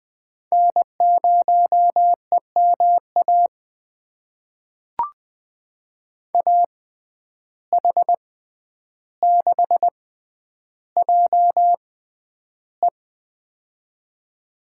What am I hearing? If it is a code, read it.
Morse code, heard as a keyed tone, about 20 wpm, spaced as for 10 wpm, 700 Hz. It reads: N0EMA AH6JE
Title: USA Call Signs - 5 Characters - 6x Character Spacing - 20wpm